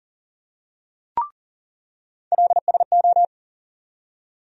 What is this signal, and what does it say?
Morse code, heard as a keyed tone, about 40 wpm, 700 Hz. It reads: LSO
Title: USA Call Sign Postfixes - 3 Letters 40wpm